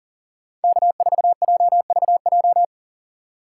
Morse code, heard as a keyed tone, about 38 wpm, 700 Hz. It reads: K4JVJ
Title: USA Call Signs - 5 Characters - 6x Character Spacing - 40wpm